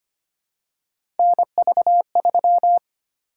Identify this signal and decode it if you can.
Morse code, heard as a keyed tone, about 25 wpm, 700 Hz. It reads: NV3
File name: Three Letter-Number Combinations - 25wpm